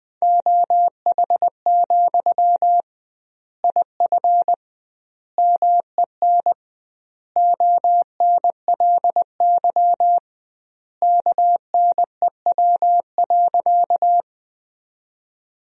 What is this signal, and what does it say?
Morse code, heard as a keyed tone, about 20 wpm, 700 Hz. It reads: OH, IF MEN ONLY KNEW.
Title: 2x Word-Spacing - Sentences from Top 500 Words - 20wpm